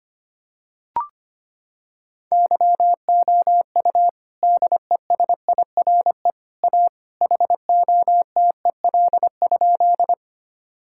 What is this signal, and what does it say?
Morse code, heard as a keyed tone, about 25 wpm, 700 Hz. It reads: YOU DESIRE A HOTEL?
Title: Sentences from Top 1600 Words 25wpm